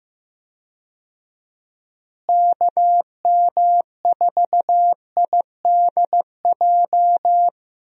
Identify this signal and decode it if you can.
Morse code, heard as a keyed tone, about 15 wpm, 700 Hz. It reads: KM4IDJ
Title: USA Call Signs - 6 Characters - Repeated 3x - 15wpm